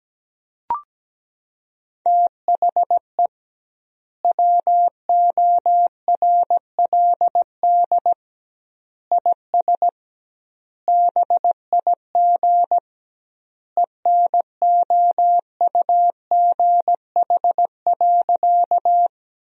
Morse code, heard as a keyed tone, about 17 wpm, 700 Hz. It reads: THE WORLD IS BIG ENOUGH.
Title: 2x Word-Spacing - Sentences from Top 400 Words - 17wpm